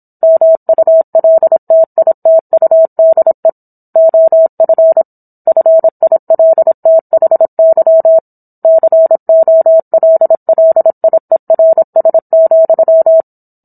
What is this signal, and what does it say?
Morse code, heard as a keyed tone, about 27 wpm, 650 Hz. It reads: MULTITUDE OF FILTHY COLLIERS,